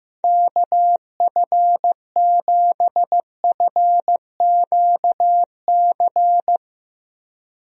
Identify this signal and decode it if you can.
Morse code, heard as a keyed tone, about 15 wpm, 700 Hz. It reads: KF7FQC
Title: USA Call Signs - 6 Characters - Repeated 2x - 15wpm